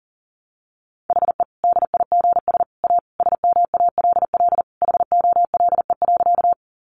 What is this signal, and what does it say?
Morse code, heard as a keyed tone, about 40 wpm, 700 Hz. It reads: HE DIGS A SMALL HOLE.
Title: Easy Phrases 40wpm